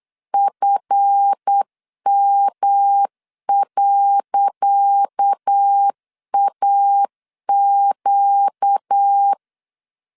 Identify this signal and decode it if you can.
Morse code, heard as a keyed tone, about 8 wpm, 800 Hz. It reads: FM.AQ
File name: LEZIONI / 10 SIGNS / 10C DOT 8wpm 3649